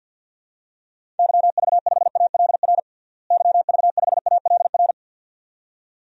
Morse code, heard as a keyed tone, about 53 wpm, 700 Hz. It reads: YV5ALR YV5ALR
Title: DXCC-100 Call Signs - 6 Characters - Repeated 2x - 50wpm